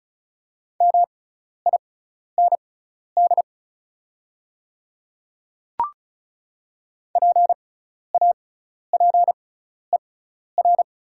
Morse code, heard as a keyed tone, about 36 wpm, spaced as for 12 wpm, 700 Hz. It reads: MIND PAPER
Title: 6x Character Spacing - Top 500 Words - 35wpm